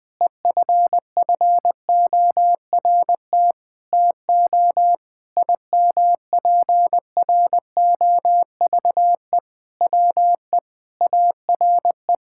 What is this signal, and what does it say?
Morse code, heard as a keyed tone, about 21 wpm, 700 Hz. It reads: EFFORT TO IMPROVE WE ARE
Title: Sentences with 6-Word Phrases - 20wpm